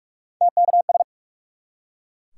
Morse code, heard as a keyed tone, about 44 wpm, 700 Hz. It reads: TKS
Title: Top 180 QSO Elements 45wpm